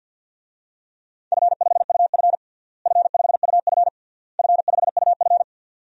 Morse code, heard as a keyed tone, about 49 wpm, 700 Hz. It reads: F5UF F5UF F5UF
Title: DXCC-100 Call Signs - 4 Characters - Repeated 3x - 50wpm